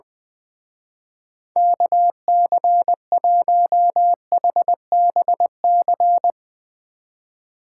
Morse code, heard as a keyed tone, about 22 wpm, 700 Hz. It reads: KC1HBC
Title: Speed Racing - USA 2x3 Call Signs 20wpm